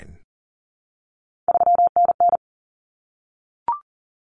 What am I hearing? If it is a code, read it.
Morse code, heard as a keyed tone, about 40 wpm, 700 Hz. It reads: 3NN
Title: Common Cut Numbers - 3-Digit 40wpm